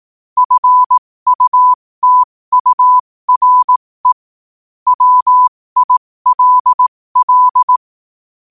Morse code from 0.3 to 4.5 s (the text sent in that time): FUTURE